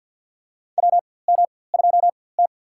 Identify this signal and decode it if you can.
Morse code, heard as a keyed tone, about 51 wpm, spaced as for 23 wpm, 700 Hz. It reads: WM2T
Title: USA Call Signs - 4x Character Spacing - 50wpm